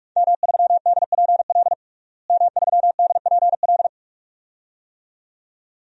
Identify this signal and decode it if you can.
Morse code, heard as a keyed tone, about 46 wpm, 700 Hz. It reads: M3DPL M3DPL
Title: DXCC-100 Call Signs - 5 Characters - Repeated 2x - 45wpm